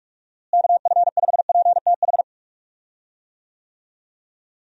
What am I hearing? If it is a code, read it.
Morse code, heard as a keyed tone, about 43 wpm, 700 Hz. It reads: KF5PTH
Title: USA Call Signs - 6 Characters - Repeated 2x - 45wpm